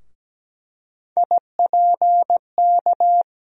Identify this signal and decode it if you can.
Morse code, heard as a keyed tone, about 17 wpm, 700 Hz. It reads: IPK
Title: USA Call Sign Postfixes - 3 Letters 17wpm